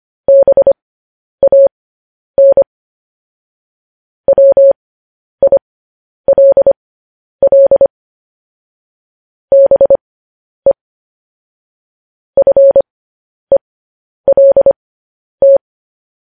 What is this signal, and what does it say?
Morse code, heard as a keyed tone, about 25 wpm, spaced as for 10 wpm, 550 Hz. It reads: BAN WILL BE FELT